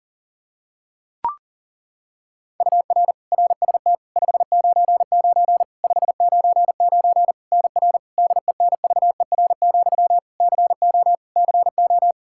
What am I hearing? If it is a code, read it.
Morse code, heard as a keyed tone, about 40 wpm, 700 Hz. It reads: UR RST 599 599 NR DENVER, CO CO
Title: Practice QSO - Exchange 1 - Beginning - 40wpm